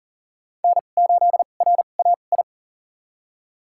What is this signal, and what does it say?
Morse code, heard as a keyed tone, about 39 wpm, spaced as for 29 wpm, 700 Hz. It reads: N8RAI
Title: USA Call Signs - 5 Characters - 2x Character Spacing - 40wpm